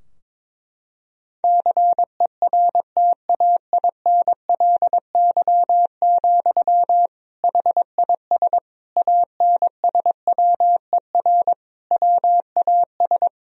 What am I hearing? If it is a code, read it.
Morse code, heard as a keyed tone, about 21 wpm, 700 Hz. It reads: CERTAINLY, HIS ANSWER WAS
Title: Sentences from Top 1000 Words - Review Part 2 - 22wpm